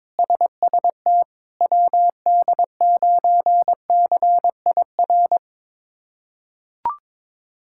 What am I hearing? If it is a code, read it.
Morse code, heard as a keyed tone, about 21 wpm, 700 Hz. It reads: SST WD9CIR
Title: K1USN SST QSO Practice - 22wpm